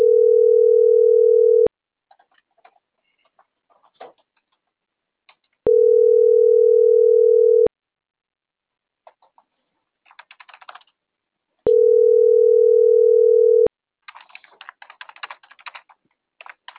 Ringback tone (North America), 440 + 480 Hz.